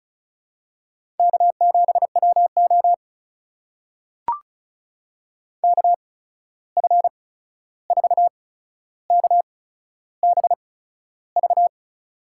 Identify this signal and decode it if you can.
Morse code, heard as a keyed tone, about 34 wpm, spaced as for 13 wpm, 700 Hz. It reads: K7WO KF4KBV
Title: USA Call Signs - 8x Character Spacing - 35wpm